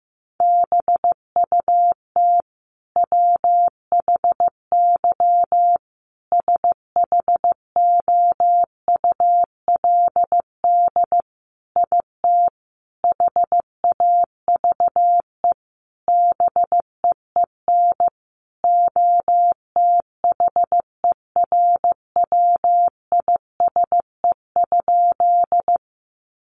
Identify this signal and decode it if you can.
Morse code, heard as a keyed tone, about 15 wpm, 700 Hz. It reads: BUT WHY SHOULD IT HAVE BEEN OTHERWISE?
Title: Sentences from Top 1200 Words 15wpm